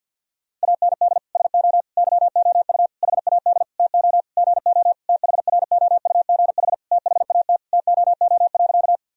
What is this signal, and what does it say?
Morse code, heard as a keyed tone, about 46 wpm, 700 Hz. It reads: AND SO YOU HAD TO GO THROUGH THAT TOO.
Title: Sentences from Top 1000 Words - Review Part 1 - 50wpm